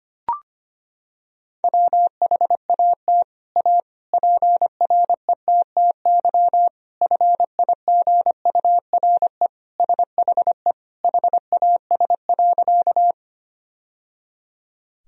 Morse code, heard as a keyed tone, about 25 wpm, 700 Hz. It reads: WHAT A PRETTY FIGURE SHE HAS.
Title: Sentences from Top 1000 Words - Review Part 2 - 25wpm